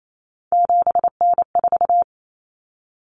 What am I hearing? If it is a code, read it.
Morse code, heard as a keyed tone, about 28 wpm, 700 Hz. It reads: 7N4
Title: Common Cut Numbers - 3-Digit 28wpm